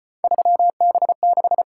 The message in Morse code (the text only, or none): 3B6